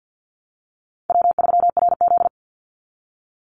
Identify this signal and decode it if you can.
Morse code, heard as a keyed tone, about 49 wpm, 700 Hz. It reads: W3RZ
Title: Easy Call Signs Repeated 3x - 50wpm